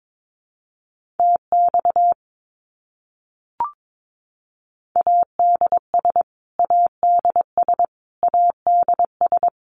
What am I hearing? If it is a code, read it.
Morse code, heard as a keyed tone, about 22 wpm, 700 Hz. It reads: TX ADS ADS ADS
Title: Abbreviations Repeated 3x - 22wpm